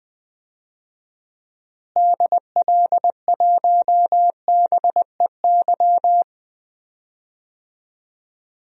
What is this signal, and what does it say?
Morse code, heard as a keyed tone, about 20 wpm, 700 Hz. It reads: DL1BEY
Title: DXCC-100 Call Signs - 6 Characters - Repeated 2x - 20wpm